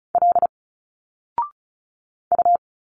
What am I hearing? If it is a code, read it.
Morse code, heard as a keyed tone, about 36 wpm, 700 Hz. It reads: L U
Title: Single Letters 35wpm